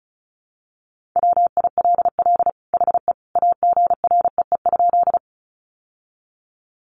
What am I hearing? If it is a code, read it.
Morse code, heard as a keyed tone, about 34 wpm, 700 Hz. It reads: WILL HE AGREE?
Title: Sentences from Top 1100 Words 35wpm